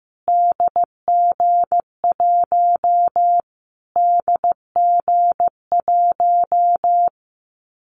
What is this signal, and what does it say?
Morse code, heard as a keyed tone, about 15 wpm, 700 Hz. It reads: DG1 DG1
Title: Callsign Prefixes for Top 25 DXCC - Repeated 3x - 15wpm